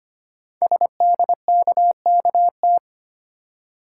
Morse code, heard as a keyed tone, about 25 wpm, 700 Hz. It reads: SDKKT
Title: Code Groups - 5 Letters - 25wpm